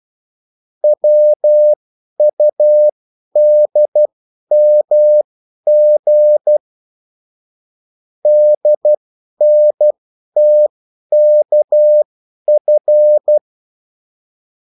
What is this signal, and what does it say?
Morse code, heard as a keyed tone, about 12 wpm, spaced as for 10 wpm, 600 Hz. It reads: WUDMG DNTKF
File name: morse mp3 / revision9-vitesse 10 mots